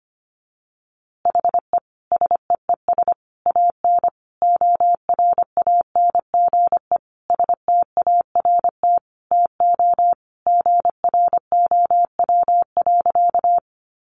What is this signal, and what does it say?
Morse code, heard as a keyed tone, about 25 wpm, 700 Hz. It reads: HE SEES AN ORANGE START TO GROW.
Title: Easy Phrases 25wpm